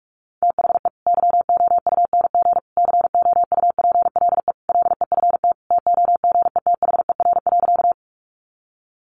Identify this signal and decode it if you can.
Morse code, heard as a keyed tone, about 44 wpm, 700 Hz. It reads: THE YOUNG COUPLE LEFT TOGETHER.
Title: Sentences from Top 1100 Words 45wpm